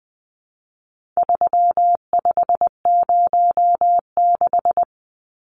Morse code, heard as a keyed tone, about 20 wpm, 700 Hz. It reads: 3506
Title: Numbers - 4-Digit 20wpm